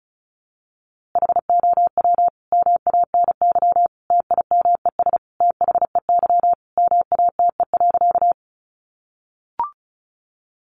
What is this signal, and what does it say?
Morse code, heard as a keyed tone, about 34 wpm, 700 Hz. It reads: HOW MANY TIMES THEY MATE.
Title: A Sound Of Thunder 35wpm